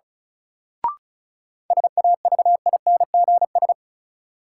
Morse code, heard as a keyed tone, about 36 wpm, 700 Hz. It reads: SAVINGS
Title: Sentences from Top 1200 Words - New Word Drill - 35wpm